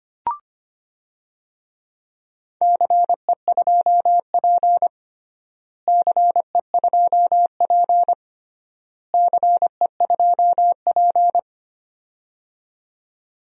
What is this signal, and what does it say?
Morse code, heard as a keyed tone, about 25 wpm, 700 Hz. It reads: CE2P CE2P CE2P
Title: DXCC-100 Call Signs - 4 Characters - Repeated 3x - 25wpm